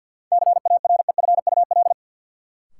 Morse code, heard as a keyed tone, about 53 wpm, 700 Hz. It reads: CAREFUL
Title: Sentences from Top 1600 Words - New Word Drill Alt - 50wpm